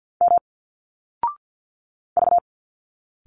Morse code, heard as a keyed tone, about 51 wpm, 700 Hz. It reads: M V
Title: Single Letters 50wpm